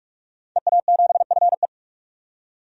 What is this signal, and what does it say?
Morse code, heard as a keyed tone, about 42 wpm, 700 Hz. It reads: EA7FE
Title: DXCC-100 Call Signs - 5 Characters - 45wpm